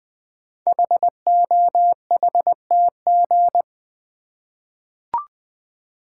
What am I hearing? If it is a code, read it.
Morse code, heard as a keyed tone, about 20 wpm, 700 Hz. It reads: HOHTG
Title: Code Groups - 5 Letters - 20wpm